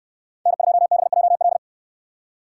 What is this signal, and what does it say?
Morse code, heard as a keyed tone, about 69 wpm, 700 Hz. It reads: N2BJL